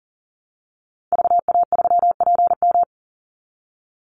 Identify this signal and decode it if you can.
Morse code, heard as a keyed tone, about 40 wpm, 700 Hz. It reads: VA3PM
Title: Call Signs - 40wpm